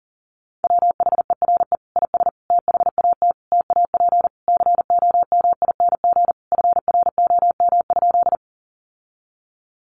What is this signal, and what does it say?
Morse code, heard as a keyed tone, about 40 wpm, 700 Hz. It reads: WHERE IS THAT TAP COMING FROM?
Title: Sentences from Top 1400 Words 40wpm